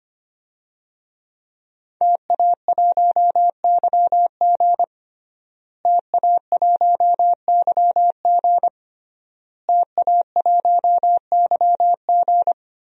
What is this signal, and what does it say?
Morse code, heard as a keyed tone, about 25 wpm, 700 Hz. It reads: TA1YG TA1YG TA1YG
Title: DXCC-100 Call Signs - 5 Characters - Repeated 3x - 25wpm